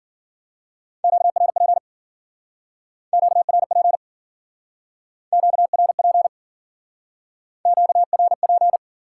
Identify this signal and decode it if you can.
Morse code, heard as a keyed tone, about 52 wpm, 700 Hz. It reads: QRP QRP QRP QRP